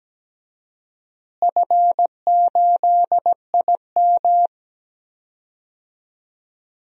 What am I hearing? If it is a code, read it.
Morse code, heard as a keyed tone, about 17 wpm, 700 Hz. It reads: F8IM
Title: DXCC-100 Call Signs - 4 Characters - Repeated 3x - 17wpm